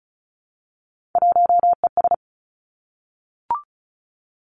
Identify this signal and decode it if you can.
Morse code, heard as a keyed tone, about 36 wpm, 700 Hz. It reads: 1ES
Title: Three Challenging Letter-Number Combinations 35wpm